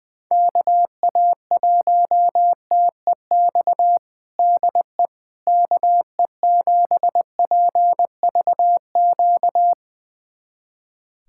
Morse code, heard as a keyed tone, about 20 wpm, 700 Hz. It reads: KA1TEX DE KE7PVQ
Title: USA Callsign DE Callsign 20wpm